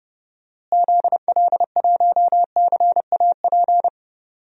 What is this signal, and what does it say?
Morse code, heard as a keyed tone, about 30 wpm, 700 Hz. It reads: ZL1CAP
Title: DXCC-100 Call Signs - 6 Characters - Repeated 2x - 30wpm